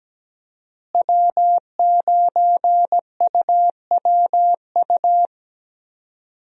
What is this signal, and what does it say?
Morse code, heard as a keyed tone, about 17 wpm, 700 Hz. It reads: W9UWU
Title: USA Call Signs - 5 Characters - Repeated 2x - 17wpm